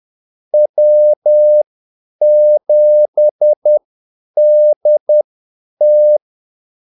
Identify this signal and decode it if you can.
Morse code, heard as a keyed tone, about 10 wpm, spaced as for 8 wpm, 600 Hz. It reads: W7DT
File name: Callsigns-10-8wpm